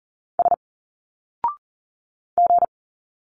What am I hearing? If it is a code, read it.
Morse code, heard as a keyed tone, about 40 wpm, 700 Hz. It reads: S G
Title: Single Letters 40wpm